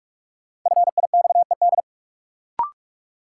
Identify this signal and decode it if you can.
Morse code, heard as a keyed tone, about 42 wpm, 700 Hz. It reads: FIXED
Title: Sentences from Top 1100 Words - New Word Drill - 45wpm